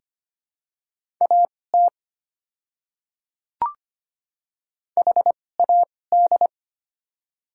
Morse code, heard as a keyed tone, about 25 wpm, spaced as for 18 wpm, 700 Hz. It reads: AT HAD
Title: 2x Character Spacing - Top 200 Words - 25wpm